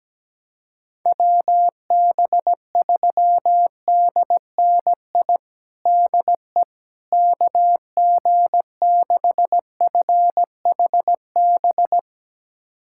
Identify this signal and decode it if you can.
Morse code, heard as a keyed tone, about 17 wpm, 700 Hz. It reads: WB3DNI DE KG6FHB
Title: USA Callsign DE Callsign - 6 Characters - 17wpm